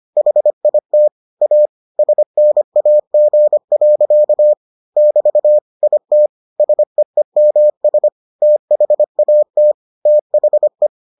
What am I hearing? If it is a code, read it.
Morse code, heard as a keyed tone, about 25 wpm, 600 Hz. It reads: HIT A SNAG. <BT> IT SEEMS THAT THE